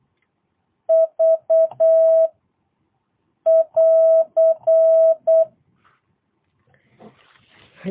Morse code, heard as a keyed tone, about 8 wpm, 650 Hz. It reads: V <AR>
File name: jh8cbh / 2022cw